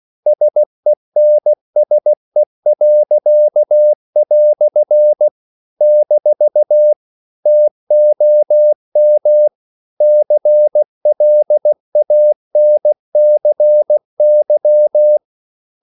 Morse code, heard as a keyed tone, about 16 wpm, 600 Hz. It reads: SENSE." - TOM CLANCY
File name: CW / Practice / EnglishQuotations0416